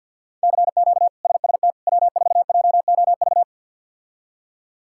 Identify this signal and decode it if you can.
Morse code, heard as a keyed tone, about 51 wpm, 700 Hz. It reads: CQ SST W4JOV
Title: K1USN SST QSO Practice - 50wpm